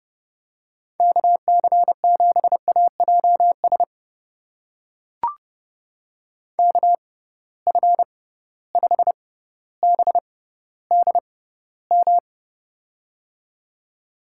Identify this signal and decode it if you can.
Morse code, heard as a keyed tone, about 31 wpm, spaced as for 16 wpm, 700 Hz. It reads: KC7AJS KF5BDM
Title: USA Call Signs - 6 Characters - 6x Character Spacing - 30wpm